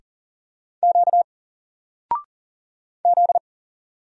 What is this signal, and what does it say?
Morse code, heard as a keyed tone, about 38 wpm, 700 Hz. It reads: Q Z
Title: Single Character (Letter:Number:Punctuation:Prosign) 40wpm